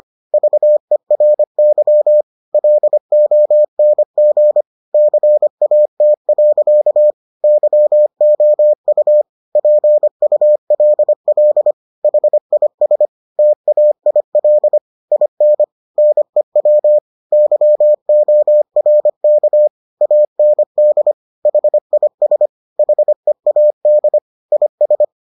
Morse code, heard as a keyed tone, about 25 wpm, 600 Hz. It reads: VERY LONG CAT. YOU PULL HIS TAIL IN NEW YORK AND HIS HEAD IS